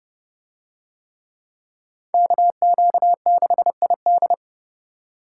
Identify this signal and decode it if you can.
Morse code, heard as a keyed tone, about 29 wpm, 700 Hz. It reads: KQ6ID